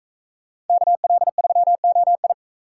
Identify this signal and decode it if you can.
Morse code, heard as a keyed tone, about 40 wpm, 700 Hz. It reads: KL3OI